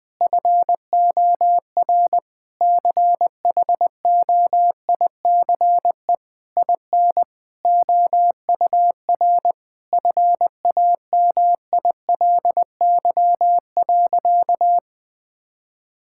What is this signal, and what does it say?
Morse code, heard as a keyed tone, about 21 wpm, 700 Hz. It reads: FOR CHOICE IN OUR FAMILY.
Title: Sentences from Top 1000 Words - Review Part 6 - 20wpm